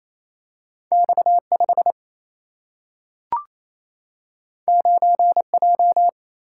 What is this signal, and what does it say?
Morse code, heard as a keyed tone, about 28 wpm, 700 Hz. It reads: X 5 9 J
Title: Single-Letter Single-Number 28wpm